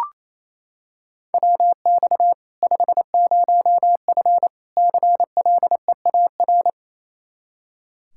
Morse code, heard as a keyed tone, about 28 wpm, 700 Hz. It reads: WX 50F CLEAR
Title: Practice QSO - Exchange 2 - Info - 28wpm